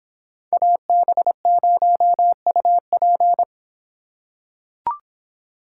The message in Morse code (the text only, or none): AB0UP